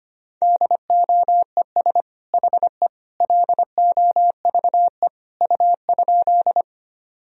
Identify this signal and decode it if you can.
Morse code, heard as a keyed tone, about 25 wpm, 700 Hz. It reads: DOES HE LOVE U?
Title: Sentences from Top 400 Words and Abbreviations 25wpm